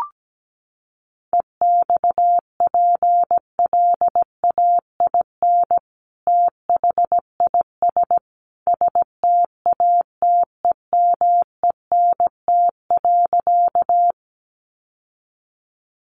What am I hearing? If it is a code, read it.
Morse code, heard as a keyed tone, about 17 wpm, 700 Hz. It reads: EXPLAIN THIS STATEMENT.
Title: Sentences from Top 1200 Words 17wpm